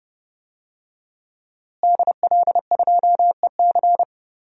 Morse code, heard as a keyed tone, about 30 wpm, 700 Hz. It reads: DL2EC
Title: DXCC-100 Call Signs - 5 Characters - Repeated 3x - 30wpm